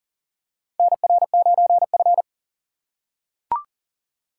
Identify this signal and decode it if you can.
Morse code, heard as a keyed tone, about 43 wpm, 690 Hz. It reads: NR9F
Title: USA Call Signs - 4 Characters - 6x Character Spacing - 40wpm